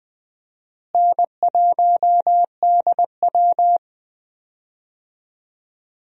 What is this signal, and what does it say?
Morse code, heard as a keyed tone, about 20 wpm, 700 Hz. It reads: N1DW
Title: USA Call Signs - 4 Characters - Repeated 2x - 20wpm